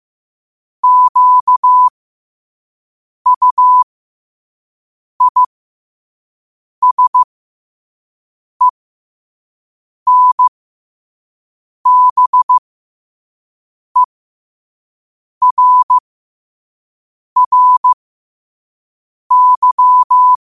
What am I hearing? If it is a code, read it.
Morse code, heard as a keyed tone, about 15 wpm, spaced as for 5 wpm, 1000 Hz. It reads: QUISENBERRY
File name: qotdfiles / 5wpm